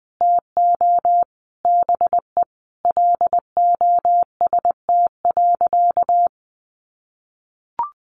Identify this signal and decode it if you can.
Morse code, heard as a keyed tone, about 20 wpm, 700 Hz. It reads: TO BE LOST.
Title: Sentences from Top 1400 Words 20wpm